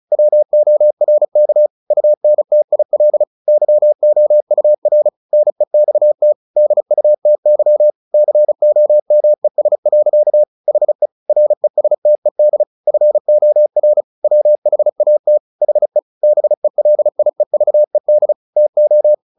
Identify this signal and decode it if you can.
Morse code, heard as a keyed tone, about 34 wpm, 600 Hz. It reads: WORK UNTIL YOUR NEXT DUTY COMES. HE RESTED FOR WHAT HE BELIEVED TO
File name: the old man and the sea 2 35wpm